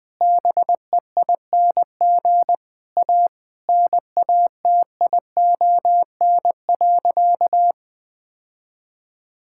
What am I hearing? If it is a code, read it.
Morse code, heard as a keyed tone, about 20 wpm, 700 Hz. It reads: BEING A NATION.
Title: Sentences from Top 1000 Words - Review Part 6 - 20wpm